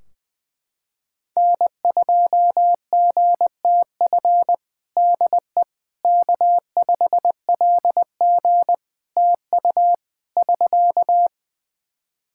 Morse code, heard as a keyed tone, about 20 wpm, 700 Hz. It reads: N2GTF DE K5LG TU <SK>
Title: Practice QSO - Exchange 4 - Ending - 20wpm